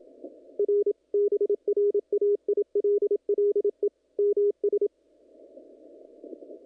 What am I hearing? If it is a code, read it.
Morse code, heard as a keyed tone, about 27 wpm, 410 Hz. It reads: R BRAILLE MS
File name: contests / naqp-cw-k5tr-op-k5pi- -1900